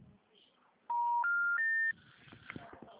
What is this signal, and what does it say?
Special information tone, with speech.